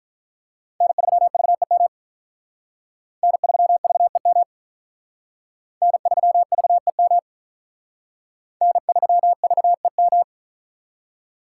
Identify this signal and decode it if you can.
Morse code, heard as a keyed tone, about 43 wpm, 700 Hz. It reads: N3VEM N3VEM N3VEM N3VEM
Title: Speed Racing - USA 1x3 Call Signs 35wpm